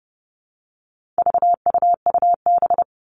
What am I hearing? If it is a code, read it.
Morse code, heard as a keyed tone, about 30 wpm, 700 Hz. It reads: VUUB
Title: All Cut Numbers - 4-Digit 30wpm